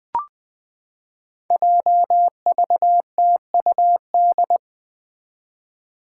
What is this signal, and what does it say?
Morse code, heard as a keyed tone, about 20 wpm, 700 Hz. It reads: JVTUD
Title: Five Challenging Letter-Number Combinations 20wpm